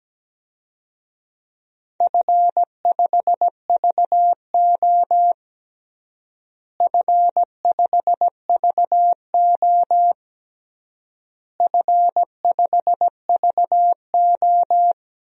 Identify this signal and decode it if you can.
Morse code, heard as a keyed tone, about 17 wpm, 700 Hz. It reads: F5VO F5VO F5VO
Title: DXCC-100 Call Signs - 4 Characters - Repeated 3x - 17wpm